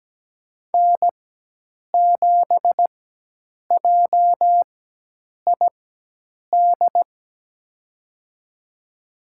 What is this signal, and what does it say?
Morse code, heard as a keyed tone, about 17 wpm, spaced as for 8 wpm, 700 Hz. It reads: N7JID
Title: USA Call Signs - 4x Character Spacing - 17wpm